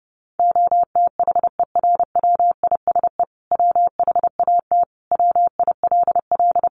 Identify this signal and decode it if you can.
Morse code, heard as a keyed tone, about 28 wpm, 700 Hz. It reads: OTHERWISE WHAT WILL
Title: Sentences from Top 1200 Words 30wpm